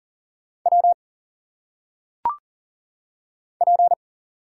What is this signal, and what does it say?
Morse code, heard as a keyed tone, about 38 wpm, 700 Hz. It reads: W P
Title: Single Letters - Phonetic Pronunciation 40wpm